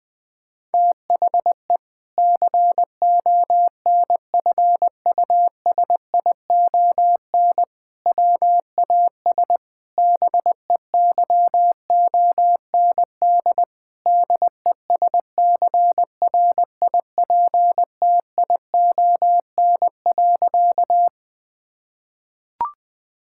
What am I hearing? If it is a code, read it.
Morse code, heard as a keyed tone, about 20 wpm, 700 Hz. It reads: THE CONFUSION WAS BEYOND DESCRIPTION.
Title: Sentences from Top 1600 Words 20wpm